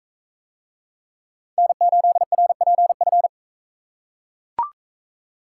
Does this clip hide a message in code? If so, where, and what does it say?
Yes: Morse code from 1.6 to 3.3 s, heard as a keyed tone, about 41 wpm, 700 Hz. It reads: N8RPF